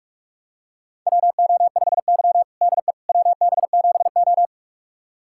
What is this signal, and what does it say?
Morse code, heard as a keyed tone, about 46 wpm, 700 Hz. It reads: WO5Y DE WB7O
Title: USA Callsign DE Callsign - 4 Characters - 45wpm